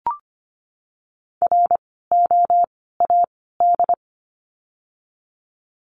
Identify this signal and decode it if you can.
Morse code, heard as a keyed tone, about 25 wpm, 700 Hz. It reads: R O A D
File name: course / 25:10wpm / 094 Lesson 16f - Letter c - Words 25wpm